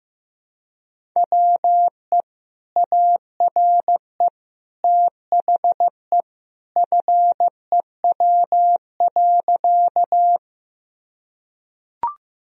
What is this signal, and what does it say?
Morse code, heard as a keyed tone, about 15 wpm, 700 Hz. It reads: WE ARE THE FEW.
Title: Sentences from Top 1000 Words - Review Part 1 - 15wpm